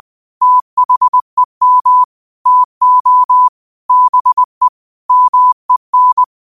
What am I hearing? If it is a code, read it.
Morse code, heard as a keyed tone, about 20 wpm, 1000 Hz. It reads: THEM TO BE MEN